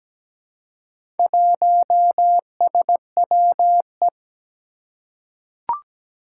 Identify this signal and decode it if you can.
Morse code, heard as a keyed tone, about 17 wpm, 700 Hz. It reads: 1SWE
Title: Four Challenging Letter-Number Combinations 17wpm